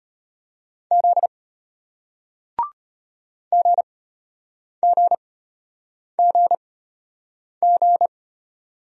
Morse code, heard as a keyed tone, about 33 wpm, spaced as for 8 wpm, 700 Hz. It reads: Z GGGG